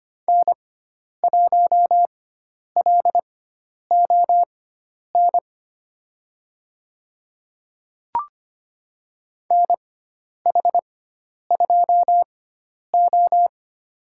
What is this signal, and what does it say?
Morse code, heard as a keyed tone, about 25 wpm, spaced as for 10 wpm, 700 Hz. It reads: N1LON NH2O
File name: Lesson 16g - Letter c - Call Signs 25wpm